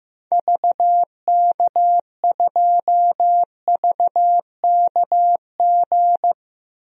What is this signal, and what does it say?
Morse code, heard as a keyed tone, about 16 wpm, 700 Hz. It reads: VK2VKG
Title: DXCC-100 Call Signs - 6 Characters - Repeated 2x - 15wpm